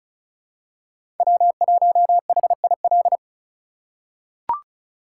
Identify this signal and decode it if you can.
Morse code, heard as a keyed tone, about 34 wpm, 700 Hz. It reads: W1HIL
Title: USA Call Signs - 5 Characters - 3x Character Spacing - 35wpm